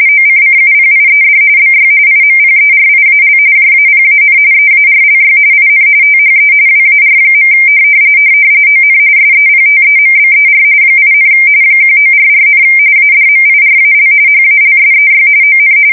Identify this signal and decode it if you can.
RTTY, about 45.45 baud. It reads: OCCUPIED BY IMPERIAL JAPAN EARLY IN THE WAR. ON SEPTEMBER 8, 1944, ADMIRAL HALSEY ON
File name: fsk170 / USS-New-Jersey